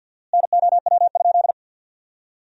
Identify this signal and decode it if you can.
Morse code, heard as a keyed tone, about 48 wpm, 700 Hz. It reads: NOW?
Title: 4x Word-Spacing - Sentences from Top 200 Words - 50wpm